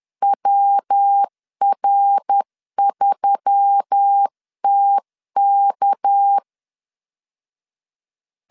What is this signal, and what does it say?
Morse code, heard as a keyed tone, about 10 wpm, 800 Hz. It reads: WR3TK
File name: CW / DET / 1 6 1500C S C 10wpm 4252